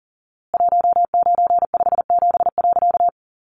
Morse code, heard as a keyed tone, about 40 wpm, 700 Hz. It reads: 1957.
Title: A Sound Of Thunder 40wpm